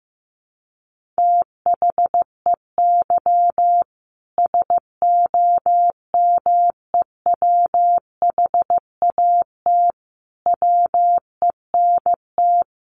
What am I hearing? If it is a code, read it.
Morse code, heard as a keyed tone, about 15 wpm, 700 Hz. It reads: THEY SOMEWHAT WENT
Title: Sentences from Top 1300 Words 15wpm